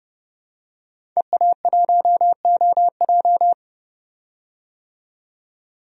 Morse code, heard as a keyed tone, about 30 wpm, 700 Hz. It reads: EA1OJ
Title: DXCC-100 Call Signs - 5 Characters - 30wpm